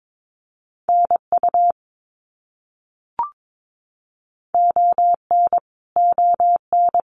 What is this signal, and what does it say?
Morse code, heard as a keyed tone, about 22 wpm, 700 Hz. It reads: NU ON ON
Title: Canadian Provinces and Territories - Repeated 3x - 22wpm